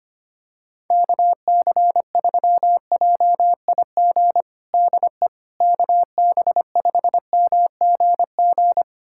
Morse code, heard as a keyed tone, about 25 wpm, 700 Hz. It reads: KC3JIG DE KB5MGG
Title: USA Callsign DE Callsign 25wpm